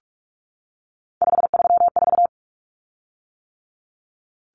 Morse code, heard as a keyed tone, about 44 wpm, 700 Hz. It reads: 534
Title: Common Cut Numbers - 3-Digit 45wpm